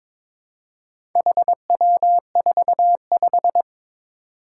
Morse code, heard as a keyed tone, about 23 wpm, 700 Hz. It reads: HW45
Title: Four Challenging Letter-Number Combinations 22wpm